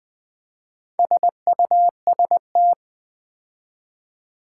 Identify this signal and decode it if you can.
Morse code, heard as a keyed tone, about 20 wpm, 700 Hz. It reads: SUST
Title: Build A Word - 4 Letters At A Time - 11-Letter Words - 20wpm